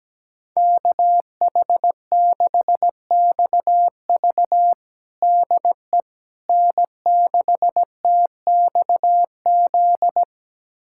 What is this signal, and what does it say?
Morse code, heard as a keyed tone, about 17 wpm, 700 Hz. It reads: KH6XV DE N6TXZ
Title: USA Callsign DE Callsign - 5 Characters - 17wpm